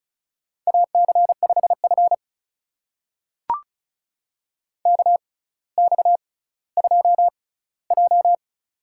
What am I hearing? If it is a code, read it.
Morse code, heard as a keyed tone, about 36 wpm, spaced as for 18 wpm, 700 Hz. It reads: AC5F KX2J
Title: USA Call Signs - 4 Characters - 6x Character Spacing - 35wpm